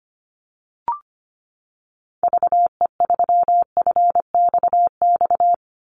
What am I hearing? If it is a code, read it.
Morse code, heard as a keyed tone, about 25 wpm, 700 Hz. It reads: VE3FXX
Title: Call Signs Repeated 2x - 25wpm